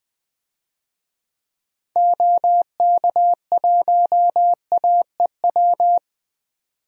Morse code, heard as a keyed tone, about 21 wpm, 700 Hz. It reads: OK1AEW
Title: DXCC-100 Call Signs - 6 Characters - 20wpm